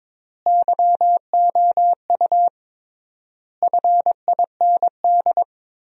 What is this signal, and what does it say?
Morse code, heard as a keyed tone, about 22 wpm, 700 Hz. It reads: YOU FIND
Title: 3x Word-Spacing - Sentences from Top 200 Words - 22wpm